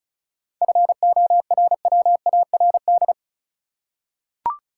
Morse code, heard as a keyed tone, about 34 wpm, 700 Hz. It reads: FORWARD
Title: Seven Letter Words - 35wpm